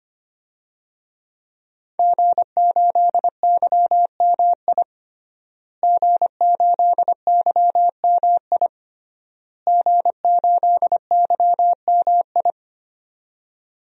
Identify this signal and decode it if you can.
Morse code, heard as a keyed tone, about 25 wpm, 700 Hz. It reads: G8YMI G8YMI G8YMI
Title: DXCC-100 Call Signs - 5 Characters - Repeated 3x - 25wpm